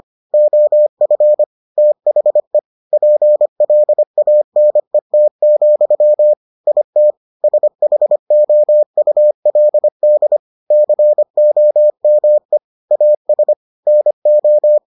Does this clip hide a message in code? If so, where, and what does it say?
Yes: Morse code from 0.3 to 14.8 s, heard as a keyed tone, about 25 wpm, 600 Hz. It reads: OF THE PLANET, IT SHOULD COME AS NO